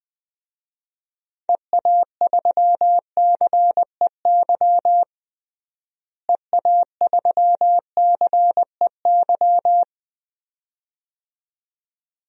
Morse code, heard as a keyed tone, about 20 wpm, 700 Hz. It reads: EA3CEY EA3CEY
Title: DXCC-100 Call Signs - 6 Characters - Repeated 2x - 20wpm